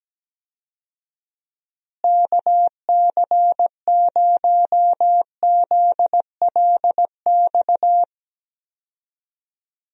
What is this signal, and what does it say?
Morse code, heard as a keyed tone, about 17 wpm, 700 Hz. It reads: KC0ZLX
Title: USA Call Signs - 6 Characters - Repeated 2x - 17wpm